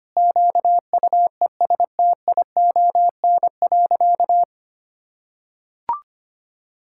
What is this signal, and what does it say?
Morse code, heard as a keyed tone, about 25 wpm, 700 Hz. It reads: QUESTION.
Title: Sentences from Top 1600 Words 25wpm